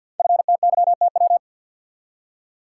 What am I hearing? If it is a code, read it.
Morse code, heard as a keyed tone, about 48 wpm, 690 Hz. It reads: FTYTW
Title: Code Groups - 5 Letters - 50wpm